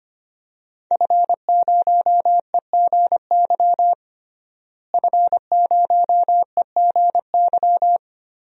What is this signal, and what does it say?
Morse code, heard as a keyed tone, about 25 wpm, 700 Hz. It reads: F0EGY F0EGY
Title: DXCC-100 Call Signs - 5 Characters - Repeated 2x - 25wpm